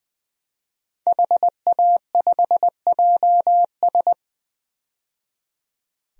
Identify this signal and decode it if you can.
Morse code, heard as a keyed tone, about 20 wpm, 700 Hz. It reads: HA5JS
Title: DXCC-100 Call Signs - 5 Characters - 20wpm